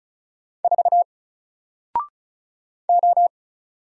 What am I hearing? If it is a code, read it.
Morse code, heard as a keyed tone, about 36 wpm, 700 Hz. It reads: 4 O
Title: Single Letter-Number Alt 35wpm